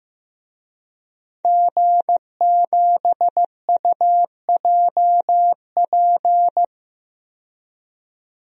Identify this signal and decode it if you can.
Morse code, heard as a keyed tone, about 15 wpm, 700 Hz. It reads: G7UJP
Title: DXCC-100 Call Signs - 5 Characters - 15wpm